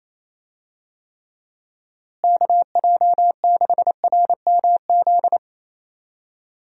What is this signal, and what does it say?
Morse code, heard as a keyed tone, about 28 wpm, 700 Hz. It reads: KJ6RMZ
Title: USA Call Signs - 6 Characters - Repeated 3x - 28wpm